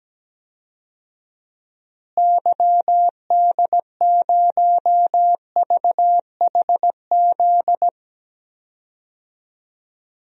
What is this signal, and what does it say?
Morse code, heard as a keyed tone, about 17 wpm, 700 Hz. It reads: YD0VHZ
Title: DXCC-100 Call Signs - 6 Characters - Repeated 2x - 17wpm